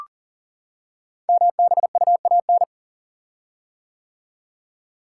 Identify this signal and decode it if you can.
Morse code, heard as a keyed tone, about 39 wpm, 700 Hz. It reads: MBUAN
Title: Code Groups - 5 Letters - 40wpm